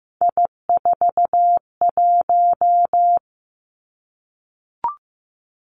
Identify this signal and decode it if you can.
Morse code, heard as a keyed tone, about 15 wpm, 700 Hz. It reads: I41
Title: Three Challenging Letter-Number Combinations 15wpm